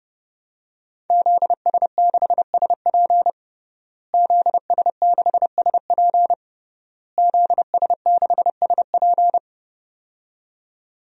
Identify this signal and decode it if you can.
Morse code, heard as a keyed tone, about 32 wpm, 700 Hz. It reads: ZS6SP ZS6SP ZS6SP
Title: DXCC-100 Call Signs - 5 Characters - Repeated 3x - 30wpm